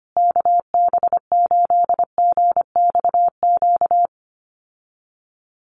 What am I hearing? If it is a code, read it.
Morse code, heard as a keyed tone, about 25 wpm, 700 Hz. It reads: KB8GXQ
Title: USA Call Signs Repeated 2x 25wpm